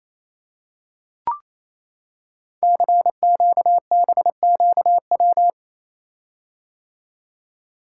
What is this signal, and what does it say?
Morse code, heard as a keyed tone, about 27 wpm, 700 Hz. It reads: CQBQW
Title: Code Groups - 5 Letter:Numbers - 28wpm